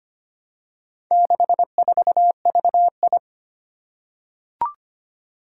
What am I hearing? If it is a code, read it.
Morse code, heard as a keyed tone, about 25 wpm, 700 Hz. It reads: 64VI
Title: Four Challenging Letter-Number Combinations 25wpm